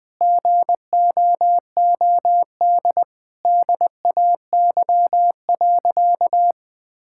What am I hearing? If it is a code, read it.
Morse code, heard as a keyed tone, about 21 wpm, 700 Hz. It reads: GOOD DAY.
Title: Sentences with 2-Word Phrases - 20wpm